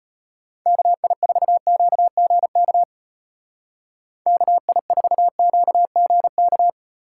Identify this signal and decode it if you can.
Morse code, heard as a keyed tone, about 32 wpm, 700 Hz. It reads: KI4QGK KI4QGK